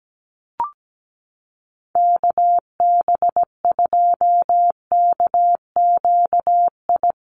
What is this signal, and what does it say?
Morse code, heard as a keyed tone, about 17 wpm, 700 Hz. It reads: KB2KQI
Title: USA Call Signs Repeated 3x 17wpm